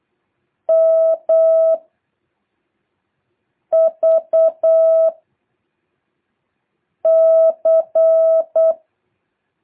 Morse code, heard as a keyed tone, about 8 wpm, 660 Hz. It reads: M V C
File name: jh8cbh / 2023cw